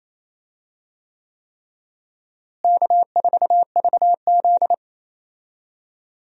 Morse code, heard as a keyed tone, about 28 wpm, 700 Hz. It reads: K4VZ
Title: USA Call Signs - 4 Characters - Repeated 3x - 28wpm